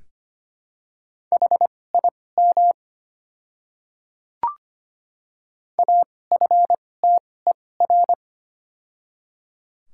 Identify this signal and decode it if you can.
Morse code, heard as a keyed tone, about 25 wpm, spaced as for 18 wpm, 690 Hz. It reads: HIM AFTER
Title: 2x Character Spacing - Top 200 Words - 25wpm